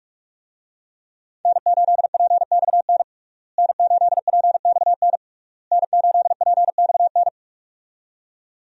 Morse code, heard as a keyed tone, about 46 wpm, 700 Hz. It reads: N8PXN N8PXN N8PXN
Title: USA Call Signs - 5 Characters - Repeated 3x - 45wpm